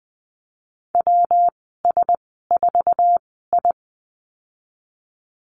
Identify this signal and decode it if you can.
Morse code, heard as a keyed tone, about 20 wpm, spaced as for 15 wpm, 700 Hz. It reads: WS4I
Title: 128 Lesson 21g - Number 6 - Call Signs 20wpm